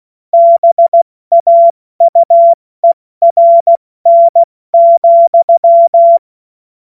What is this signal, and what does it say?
Morse code, heard as a keyed tone, about 16 wpm, spaced as for 14 wpm, 680 Hz. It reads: BAUERN,
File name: Witz-16-September-2025-70-F80